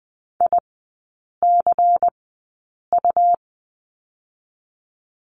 Morse code, heard as a keyed tone, about 20 wpm, spaced as for 8 wpm, 700 Hz. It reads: ICU
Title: 152 Lesson 25e - Letter p - Three Characters 20wpm